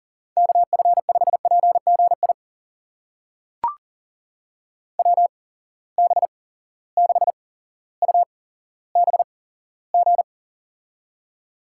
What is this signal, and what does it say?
Morse code, heard as a keyed tone, about 38 wpm, spaced as for 17 wpm, 700 Hz. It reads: KF5PGI WB6UBG
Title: USA Call Signs - 6 Characters - 8x Character Spacing - 40wpm